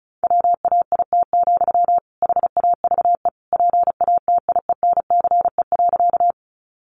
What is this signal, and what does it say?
Morse code, heard as a keyed tone, about 34 wpm, 700 Hz. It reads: WAIT, HAVE PATIENCE.
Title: Sentences from Top 1300 Words 35wpm